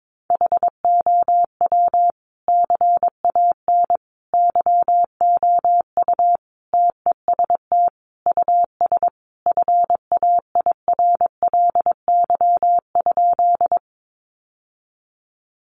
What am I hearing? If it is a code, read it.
Morse code, heard as a keyed tone, about 22 wpm, 700 Hz. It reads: HOW CAN YOU TEST US FAIRLY?
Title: Sentences from Top 1100 Words 22wpm